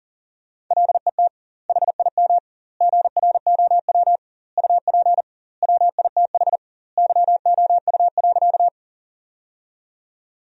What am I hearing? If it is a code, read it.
Morse code, heard as a keyed tone, about 40 wpm, 700 Hz. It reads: LET HIM GROW UP WITH YOU.
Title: 2x Word-Spacing - Sentences from Top 300 Words - 40wpm